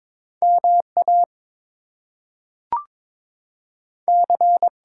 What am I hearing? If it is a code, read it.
Morse code, heard as a keyed tone, about 22 wpm, 700 Hz. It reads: MA C
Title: US States and Canadian Provinces 22wpm